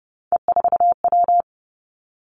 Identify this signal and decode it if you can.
Morse code, heard as a keyed tone, about 30 wpm, 700 Hz. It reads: E4W
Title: Three Challenging Letter-Number Combinations 30wpm